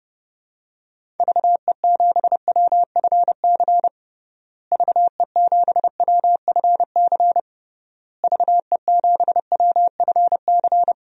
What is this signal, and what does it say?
Morse code, heard as a keyed tone, about 30 wpm, 700 Hz. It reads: VE7WFC VE7WFC VE7WFC
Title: DXCC-100 Call Signs - 6 Characters - Repeated 3x - 30wpm